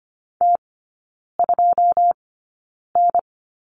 Morse code, heard as a keyed tone, about 25 wpm, spaced as for 9 wpm, 700 Hz. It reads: T2N
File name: Lesson 14e - Number 2 - Three Characters 25wpm